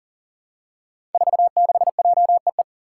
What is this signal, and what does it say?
Morse code, heard as a keyed tone, about 38 wpm, 700 Hz. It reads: 46JEE
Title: Five Challenging Letter-Number Combinations 40wpm